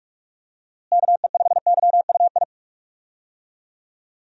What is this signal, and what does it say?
Morse code, heard as a keyed tone, about 41 wpm, 700 Hz. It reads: KE5YUI
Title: USA Call Signs - 6 Characters - Repeated 2x - 45wpm